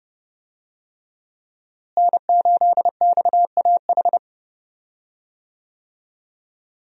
Morse code, heard as a keyed tone, about 28 wpm, 700 Hz. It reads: N8XAH
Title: USA Call Signs - 5 Characters - Repeated 3x - 30wpm